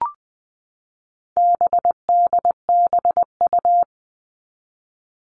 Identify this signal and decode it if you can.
Morse code, heard as a keyed tone, about 20 wpm, 700 Hz. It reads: BDBU
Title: All Cut Numbers - 4-Digit 20wpm